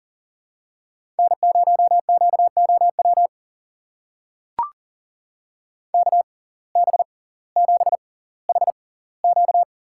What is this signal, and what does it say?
Morse code, heard as a keyed tone, about 40 wpm, spaced as for 20 wpm, 700 Hz. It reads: N0QOW KB7HQ
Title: USA Call Signs - 6x Character Spacing - 40wpm